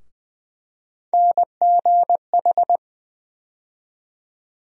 Morse code, heard as a keyed tone, about 20 wpm, 700 Hz. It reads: NGH
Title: USA Call Sign Postfixes - 3 Letters 20wpm